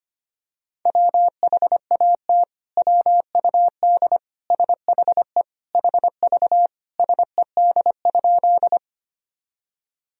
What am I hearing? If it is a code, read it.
Morse code, heard as a keyed tone, about 25 wpm, 700 Hz. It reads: WHAT WUD SHE HV SED?
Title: Sentences from Top 100 Words and Abbreviations 25wpm